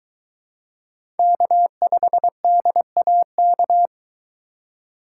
Morse code, heard as a keyed tone, about 23 wpm, 700 Hz. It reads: K5DAK